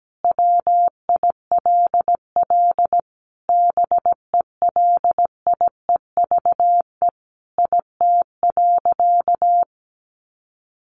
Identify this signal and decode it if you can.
Morse code, heard as a keyed tone, about 17 wpm, 700 Hz. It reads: WILL BELIEVE IT.
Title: Sentences from Top 1000 Words 17wpm